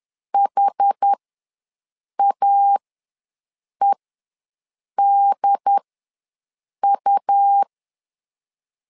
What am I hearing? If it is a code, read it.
Morse code, heard as a keyed tone, about 10 wpm, spaced as for 6 wpm, 800 Hz. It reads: HAEDU
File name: LEZIONI / 4 Q ws 10wpm 2306